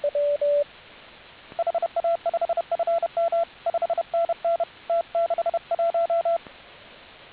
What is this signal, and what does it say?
Morse code, heard as a keyed tone, about 31 wpm, 680 Hz, with radio noise. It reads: HA5FM 5NN T61